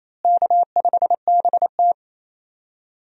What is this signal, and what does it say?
Morse code, heard as a keyed tone, about 27 wpm, 700 Hz. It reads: K5BT
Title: USA Call Signs - 4 Characters - Repeated 3x - 28wpm